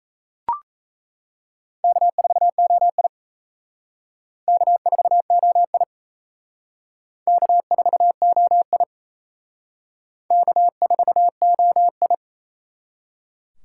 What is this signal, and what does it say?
Morse code, heard as a keyed tone, about 36 wpm, 700 Hz. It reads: K4OI K4OI K4OI K4OI